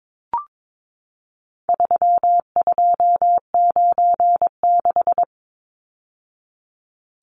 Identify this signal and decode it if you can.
Morse code, heard as a keyed tone, about 22 wpm, 700 Hz. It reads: 3296
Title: Numbers - 4-Digit 22wpm